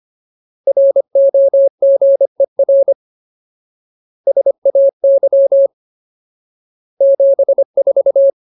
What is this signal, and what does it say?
Morse code, heard as a keyed tone, about 25 wpm, 550 Hz. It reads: ROGER SAY 74
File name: Ultimate Word List / 25WPM Ultimate